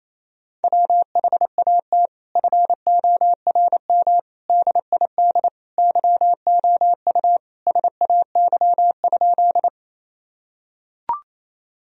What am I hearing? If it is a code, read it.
Morse code, heard as a keyed tone, about 28 wpm, 700 Hz. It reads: WHAT FORM DID YOU SAY?
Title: Sentences from Top 1000 Words - Review Part 1 - 28wpm